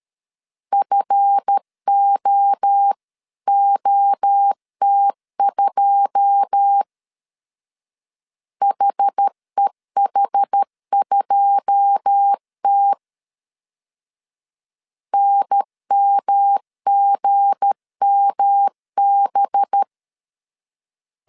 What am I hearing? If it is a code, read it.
Morse code, heard as a keyed tone, about 13 wpm, 800 Hz. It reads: FOOT2 HEH2T NMGMB
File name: CW / DET / 1 9 1500C S A 12wpm 3800